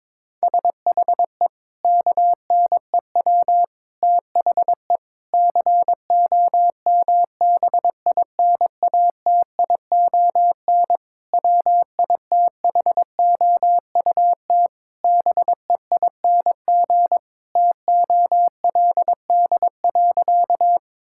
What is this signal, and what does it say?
Morse code, heard as a keyed tone, about 22 wpm, 700 Hz. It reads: SHE KNEW THE COMBINATION WITHOUT BEING TOLD.
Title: Sentences from Top 1000 Words - Review Part 6 - 22wpm